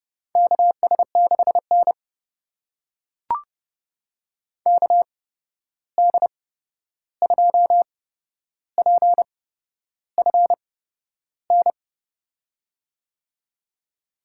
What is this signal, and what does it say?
Morse code, heard as a keyed tone, about 30 wpm, spaced as for 11 wpm, 700 Hz. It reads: KS6N KD2PFN
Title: USA Call Signs - 8x Character Spacing - 30wpm